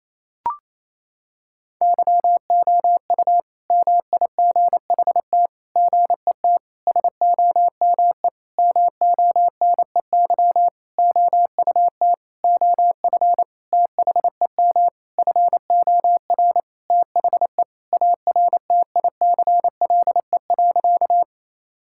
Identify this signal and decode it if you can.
Morse code, heard as a keyed tone, about 28 wpm, 700 Hz. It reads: YOU MIGHT GET SOME MONEY OUT OF THEM FOR THE ARTICLE.
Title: Sentences from Top 1000 Words - Review Part 5 - 28wpm